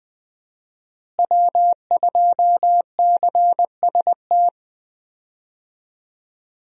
Morse code, heard as a keyed tone, about 21 wpm, 700 Hz. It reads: W2CST
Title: USA Call Signs - 5 Characters - Repeated 2x - 20wpm